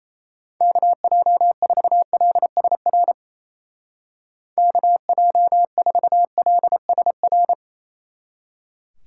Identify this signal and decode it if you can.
Morse code, heard as a keyed tone, about 34 wpm, 700 Hz. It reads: KJ4LSR KJ4LSR
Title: Speed Racing - USA 2x3 Call Signs 28wpm